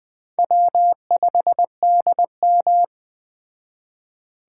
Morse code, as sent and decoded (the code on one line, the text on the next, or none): .-- ..... -.. --
W5DM